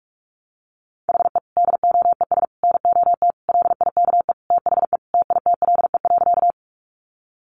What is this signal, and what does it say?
Morse code, heard as a keyed tone, about 44 wpm, 700 Hz. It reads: HE DOES NOT LIKE THE TITLE.
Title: Sentences from Top 1200 Words 45wpm